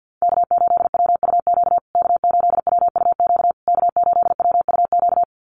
Morse code, heard as a keyed tone, about 50 wpm, 700 Hz. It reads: K8WUQ K8WUQ K8WUQ
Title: USA Call Signs Repeated 3x 50wpm